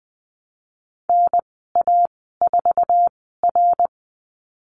Morse code, heard as a keyed tone, about 20 wpm, spaced as for 15 wpm, 700 Hz. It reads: NA4R
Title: 088 Lesson 15g - Number 5 - Call Signs 20wpm